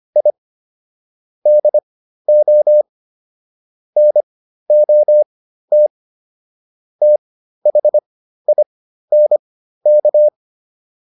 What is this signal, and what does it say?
Morse code, heard as a keyed tone, about 25 wpm, spaced as for 13 wpm, 600 Hz. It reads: I DO NOT THINK